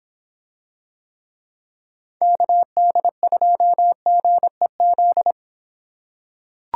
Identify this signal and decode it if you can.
Morse code, heard as a keyed tone, about 26 wpm, 700 Hz. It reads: KD2GEZ